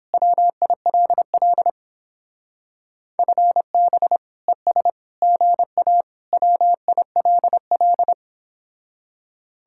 Morse code, heard as a keyed tone, about 31 wpm, 700 Hz. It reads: WILL FB ES GA WILL